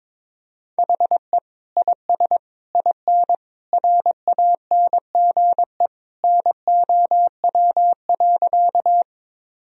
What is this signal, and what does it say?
Morse code, heard as a keyed tone, about 22 wpm, 700 Hz. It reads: HE IS IN RANGE NOW.
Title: Sentences from Top 1000 Words - Review Part 4 - 22wpm